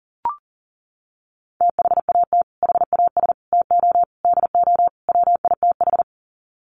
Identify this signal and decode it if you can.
Morse code, heard as a keyed tone, about 37 wpm, 700 Hz. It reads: THAT HAS TO DO WITH
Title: Sets of 5 Words Encore2 40wpm